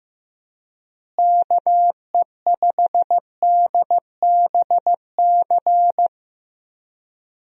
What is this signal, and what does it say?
Morse code, heard as a keyed tone, about 15 wpm, 700 Hz. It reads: KE5DBC
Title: USA Call Signs - 6 Characters - Repeated 2x - 15wpm